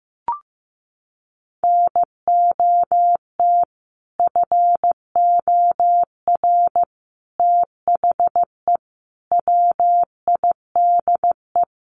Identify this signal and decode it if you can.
Morse code, heard as a keyed tone, about 15 wpm, 700 Hz. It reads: NOT FOR THE WIDE
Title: Sentences from Top 900 Words 15wpm